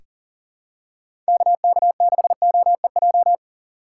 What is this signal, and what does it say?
Morse code, heard as a keyed tone, about 40 wpm, 700 Hz. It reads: KK6OEJ
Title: USA Call Signs - 6 Characters - 4x Character Spacing - 40wpm